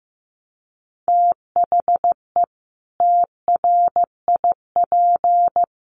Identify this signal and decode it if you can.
Morse code, heard as a keyed tone, about 15 wpm, 700 Hz. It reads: THE TRIP
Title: Sentences from Top 1300 Words 15wpm